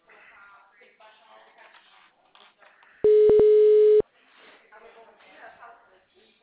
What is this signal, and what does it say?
Ringback tone (Europe, 425 Hz), with speech.